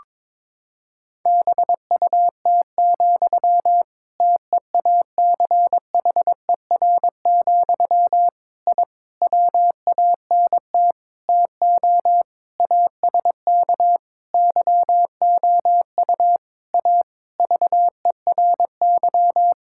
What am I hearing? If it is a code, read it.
Morse code, heard as a keyed tone, about 22 wpm, 700 Hz. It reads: BUT, TEACHER, I WANT TO ASK YOU A VERY
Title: Sentences from Top 1000 Words - Review Part 6 - 22wpm